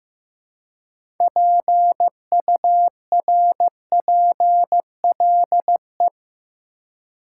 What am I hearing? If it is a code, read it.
Morse code, heard as a keyed tone, about 15 wpm, 700 Hz. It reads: PURPLE
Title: Six Letter Words - 15wpm